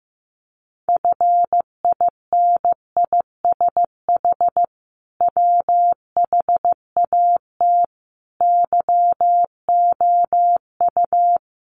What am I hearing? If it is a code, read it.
Morse code, heard as a keyed tone, about 15 wpm, 700 Hz. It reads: FINISH WHAT YOU
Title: Sentences from Top 1200 Words 15wpm